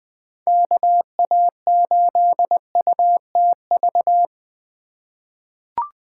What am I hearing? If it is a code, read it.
Morse code, heard as a keyed tone, about 20 wpm, 700 Hz. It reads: KA8UTV
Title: USA Call Signs - 6 Characters - 2x Character Spacing - 20wpm